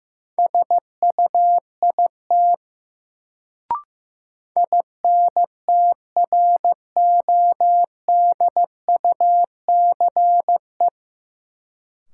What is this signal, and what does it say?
Morse code, heard as a keyed tone, about 15 wpm, 700 Hz. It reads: SUIT INTRODUCE
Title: Sentences from Top 1600 Words - New Word Drill - 15wpm